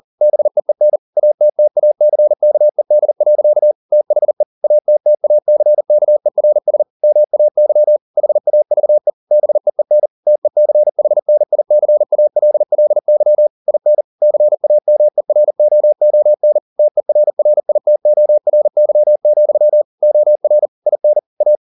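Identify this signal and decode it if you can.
Morse code, heard as a keyed tone, about 40 wpm, 600 Hz. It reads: BEEN ATTACKED. THE ATTACKERS MAY HAVE BEEN TECHNICALLY IN CAMEROON TERRITORY, OR IN A
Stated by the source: Morse code at 40 wpm, practice text, CW med Gnister.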